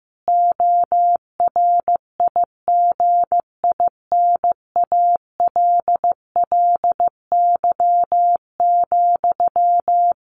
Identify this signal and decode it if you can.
Morse code, heard as a keyed tone, about 14 wpm, 700 Hz. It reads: ORIGINALLY,
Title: A Sound Of Thunder 15wpm